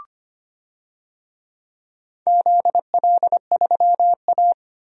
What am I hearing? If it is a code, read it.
Morse code, heard as a keyed tone, about 25 wpm, 700 Hz. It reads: ZL3A
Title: DXCC-100 Call Signs - 4 Characters - 25wpm